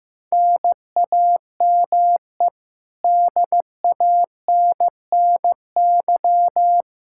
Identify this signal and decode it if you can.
Morse code, heard as a keyed tone, about 15 wpm, 700 Hz. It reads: NAME DANNY
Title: Practice QSO - Exchange 1 - Beginning - 15wpm